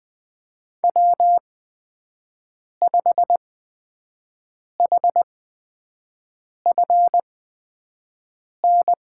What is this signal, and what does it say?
Morse code, heard as a keyed tone, about 20 wpm, spaced as for 5 wpm, 700 Hz. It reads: W5HFN
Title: USA Call Signs - 8x Character Spacing - 20wpm